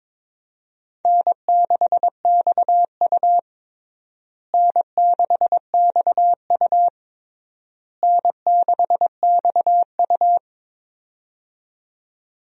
Morse code, heard as a keyed tone, about 22 wpm, 700 Hz. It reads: N6XU N6XU N6XU
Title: USA Call Signs - 4 Characters - Repeated 3x - 22wpm